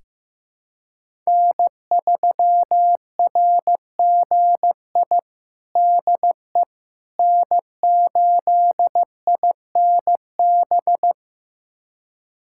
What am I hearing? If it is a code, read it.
Morse code, heard as a keyed tone, about 15 wpm, 700 Hz. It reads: N3RGI DE N8INB
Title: USA Callsign DE Callsign - 5 Characters - 15wpm